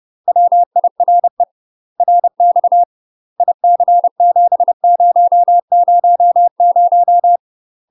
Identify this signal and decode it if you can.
Morse code, heard as a keyed tone, about 30 wpm, 700 Hz. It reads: WIRE RX IC7000
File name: UK MASTER QSOs TRAINING / 30WPM / UK MASTER QSOs 0001